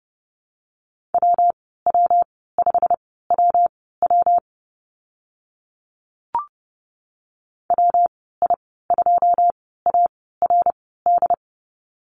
Morse code, heard as a keyed tone, about 30 wpm, spaced as for 17 wpm, 700 Hz. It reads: WW5WW WI2ARD
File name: Lesson 19g - Letter w - Call Signs 30wpm